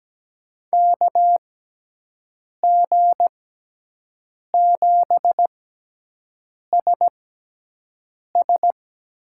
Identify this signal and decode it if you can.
Morse code, heard as a keyed tone, about 17 wpm, spaced as for 6 wpm, 700 Hz. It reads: KG7SS
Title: USA Call Signs - 6x Character Spacing - 17wpm